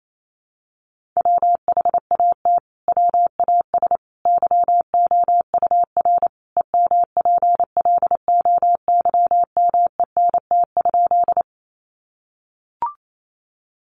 Morse code, heard as a keyed tone, about 28 wpm, 700 Hz. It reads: WHAT WAS YOUR EMPLOYMENT?
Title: Sentences from Top 1500 Words 28wpm